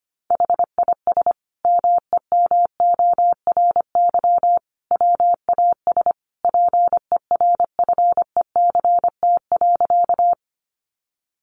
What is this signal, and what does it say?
Morse code, heard as a keyed tone, about 25 wpm, 700 Hz. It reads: HIS MEMORY WAS PERFECT.
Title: Sentences from Top 1300 Words 25wpm